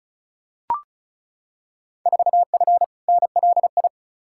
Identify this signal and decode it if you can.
Morse code, heard as a keyed tone, about 36 wpm, 700 Hz. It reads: 4F NLI
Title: Field Day Exchanges Encore - 35wpm